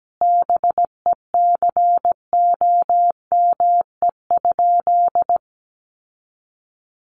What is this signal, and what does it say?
Morse code, heard as a keyed tone, about 17 wpm, 700 Hz. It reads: BECOME?
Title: Sentences from Top 600 Words 17wpm